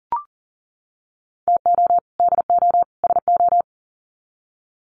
Morse code, heard as a keyed tone, about 40 wpm, 700 Hz. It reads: TO DO SO
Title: Sets of 3 Words 40wpm